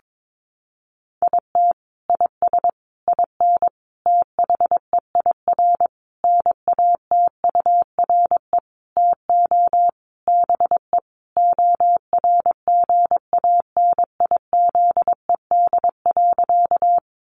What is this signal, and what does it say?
Morse code, heard as a keyed tone, about 22 wpm, 700 Hz. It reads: IT IS IN THEIR NATURE TO BE ORGANIZED.
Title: Sentences from Top 1300 Words 22wpm